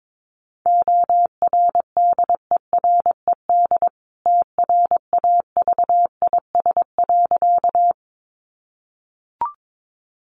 Morse code, heard as a keyed tone, about 22 wpm, 700 Hz. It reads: ORDERED TRAVIS.
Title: A Sound Of Thunder 22wpm